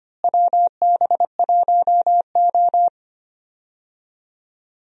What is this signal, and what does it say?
Morse code, heard as a keyed tone, about 25 wpm, 700 Hz. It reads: WB1O
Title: USA Call Signs - 4 Characters - Repeated 3x - 25wpm